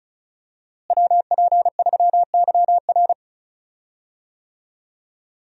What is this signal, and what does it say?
Morse code, heard as a keyed tone, about 36 wpm, 700 Hz. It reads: WP3YR
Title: USA Call Signs - 5 Characters - Repeated 2x - 35wpm